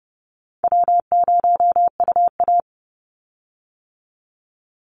Morse code, heard as a keyed tone, about 30 wpm, 700 Hz. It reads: W0UA
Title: Call Signs - 30wpm